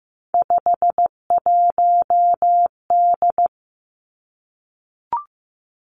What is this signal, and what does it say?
Morse code, heard as a keyed tone, about 15 wpm, 700 Hz. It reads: 51D
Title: Three Challenging Letter-Number Combinations 15wpm